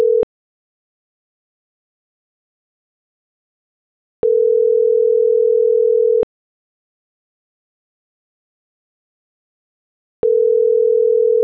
Ringback tone (North America), 440 + 480 Hz.